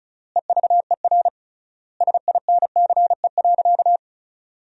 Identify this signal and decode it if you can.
Morse code, heard as a keyed tone, about 34 wpm, 700 Hz. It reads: EVER SINCE.
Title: 3x Word-Spacing - Sentences from Top 400 Words - 35wpm